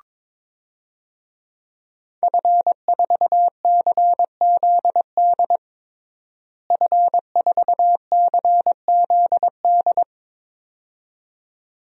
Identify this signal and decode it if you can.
Morse code, heard as a keyed tone, about 22 wpm, 700 Hz. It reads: F4CZD F4CZD
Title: DXCC-100 Call Signs - 5 Characters - Repeated 2x - 22wpm